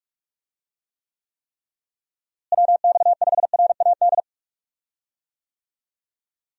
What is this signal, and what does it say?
Morse code, heard as a keyed tone, about 46 wpm, 700 Hz. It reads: WX5RAD
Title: USA Call Signs - 6 Characters - Repeated 3x - 45wpm